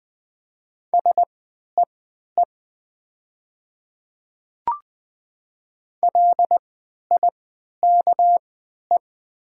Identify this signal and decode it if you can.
Morse code, heard as a keyed tone, about 20 wpm, spaced as for 11 wpm, 700 Hz. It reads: SEE LIKE
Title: 3x Character Spacing - Top 100 Words - 20wpm